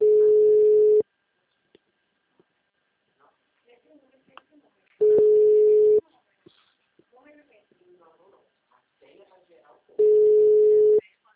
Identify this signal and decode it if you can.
Ringback tone (Europe, 425 Hz), with speech.